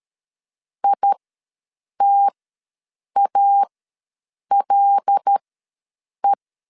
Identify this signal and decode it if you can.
Morse code, heard as a keyed tone, about 13 wpm, spaced as for 7 wpm, 800 Hz. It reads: ITALE
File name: LEZIONI / 5 LBVC / 5 L ws 12wpm 2958